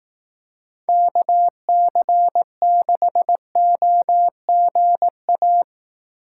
Morse code, heard as a keyed tone, about 18 wpm, 700 Hz. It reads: KC6OGA
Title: Speed Racing - USA 2x3 Call Signs 15wpm